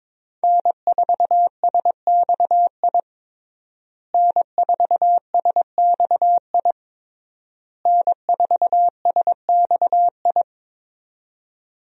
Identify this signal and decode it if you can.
Morse code, heard as a keyed tone, about 22 wpm, 700 Hz. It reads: N4SXI N4SXI N4SXI
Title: USA Call Signs - 5 Characters - Repeated 3x - 22wpm